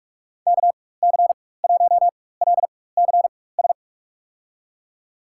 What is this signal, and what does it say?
Morse code, heard as a keyed tone, about 43 wpm, spaced as for 21 wpm, 700 Hz. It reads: KC1LCS
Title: USA Call Signs - 6 Characters - 4x Character Spacing - 45wpm